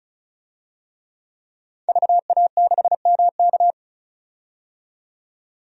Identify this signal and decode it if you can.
Morse code, heard as a keyed tone, about 36 wpm, 700 Hz. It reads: VA6MK
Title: DXCC-100 Call Signs - 5 Characters - 35wpm